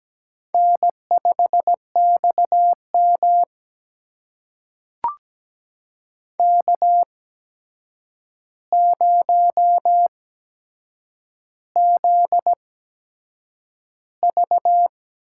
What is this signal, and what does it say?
Morse code, heard as a keyed tone, about 17 wpm, spaced as for 7 wpm, 700 Hz. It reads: N5XM K0ZV
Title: USA Call Signs - 4 Characters - 8x Character Spacing - 17wpm